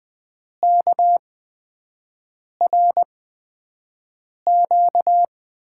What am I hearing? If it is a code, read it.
Morse code, heard as a keyed tone, about 20 wpm, spaced as for 5 wpm, 700 Hz. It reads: KRQ
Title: USA Call Signs - 5 Characters - 8x Character Spacing - 20wpm